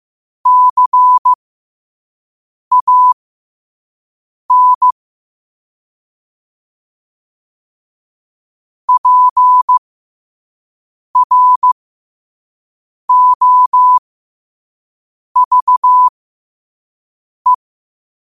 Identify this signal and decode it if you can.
Morse code, heard as a keyed tone, about 15 wpm, spaced as for 5 wpm, 1000 Hz. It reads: CAN PROVE
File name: qotdfiles / 5wpm